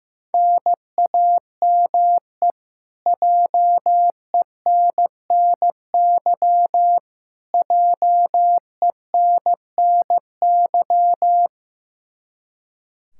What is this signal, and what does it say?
Morse code, heard as a keyed tone, about 15 wpm, 700 Hz. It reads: NAME JENNY JENNY
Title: Practice QSO - Exchange 1 - Beginning - 15wpm